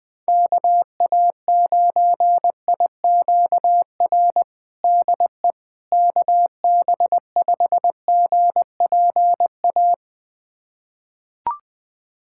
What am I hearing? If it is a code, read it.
Morse code, heard as a keyed tone, about 21 wpm, 700 Hz. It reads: KA9IQR DE KB5GPA
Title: USA Callsign DE Callsign - 6 Characters - 20wpm